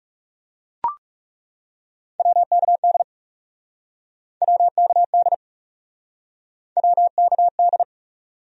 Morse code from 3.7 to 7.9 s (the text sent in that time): WKD WKD